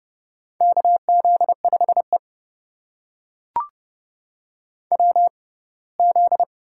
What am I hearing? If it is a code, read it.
Morse code, heard as a keyed tone, about 30 wpm, spaced as for 20 wpm, 700 Hz. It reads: KZ5E WZ
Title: USA Call Signs - 4 Characters - 6x Character Spacing - 30wpm